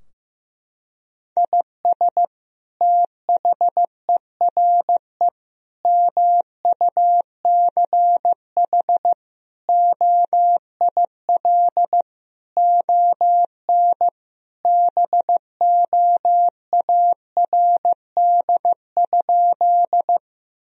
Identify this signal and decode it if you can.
Morse code, heard as a keyed tone, about 15 wpm, 700 Hz. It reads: IS THERE MUCH OIL ON BOARD?
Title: Sentences from Top 1000 Words - Review Part 4 - 15wpm